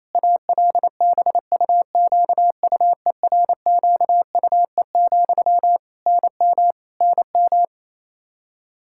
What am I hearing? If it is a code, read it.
Morse code, heard as a keyed tone, about 28 wpm, 700 Hz. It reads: ALBUQUERQUE, NM NM
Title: Practice QSO - Exchange 1 - Beginning - 28wpm